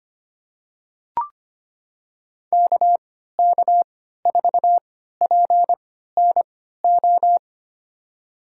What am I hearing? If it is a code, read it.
Morse code, heard as a keyed tone, about 25 wpm, spaced as for 14 wpm, 700 Hz. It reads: KK4PNO
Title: USA Call Signs - 6 Characters - 3x Character Spacing - 25wpm